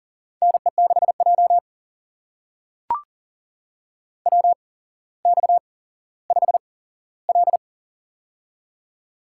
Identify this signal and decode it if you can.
Morse code, heard as a keyed tone, about 40 wpm, spaced as for 17 wpm, 700 Hz. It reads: NE6J WX5L
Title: USA Call Signs - 4 Characters - 8x Character Spacing - 40wpm